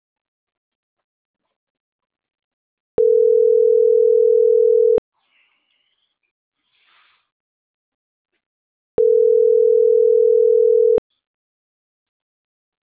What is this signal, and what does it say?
Ringback tone (North America), 440 + 480 Hz.